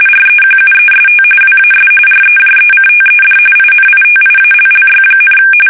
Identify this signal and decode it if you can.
RTTY, about 45.45 baud. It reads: REQUEST WAS REJECTED INITIALLY,
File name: fsk850 / usnavy3